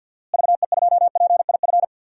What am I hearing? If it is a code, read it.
Morse code, heard as a keyed tone, about 51 wpm, 700 Hz. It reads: VE2PIF
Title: DXCC-100 Call Signs - 6 Characters - Repeated 2x - 50wpm